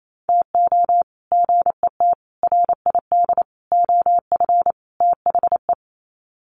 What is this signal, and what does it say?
Morse code, heard as a keyed tone, about 28 wpm, 700 Hz. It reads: TO GET RID OF THE
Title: Sets of 5 Words 28wpm